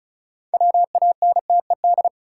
Morse code, heard as a keyed tone, about 36 wpm, 700 Hz. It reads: WANTED
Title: Sets of 2 Words - New Word Drill - 35wpm